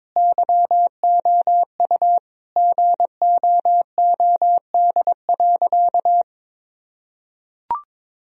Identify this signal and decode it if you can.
Morse code, heard as a keyed tone, about 22 wpm, 700 Hz. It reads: YOU GOOD.
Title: Sentences from Top 1000 Words - Review Part 1 - 22wpm